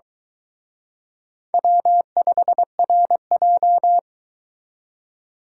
Morse code, heard as a keyed tone, about 23 wpm, 700 Hz. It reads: W5RJ